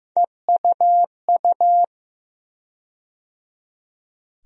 Morse code, heard as a keyed tone, about 15 wpm, 700 Hz. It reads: EUU
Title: USA Call Sign Postfixes - 3 Letters 15wpm